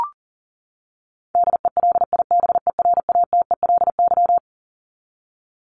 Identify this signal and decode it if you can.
Morse code, heard as a keyed tone, about 40 wpm, 700 Hz. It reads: DELIBERATELY
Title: Top 1751-2000 Words 40wpm